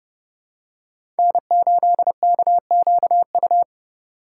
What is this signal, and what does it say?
Morse code, heard as a keyed tone, about 30 wpm, 700 Hz. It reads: N8KQU
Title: USA Call Signs - 5 Characters - 8x Character Spacing - 30wpm